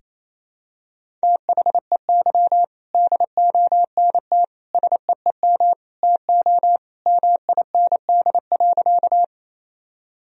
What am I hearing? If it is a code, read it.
Morse code, heard as a keyed tone, about 27 wpm, 700 Hz. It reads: THEY DONT SEEM TO MIND.
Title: Sentences from Top 1000 Words - Review Part 1 - 28wpm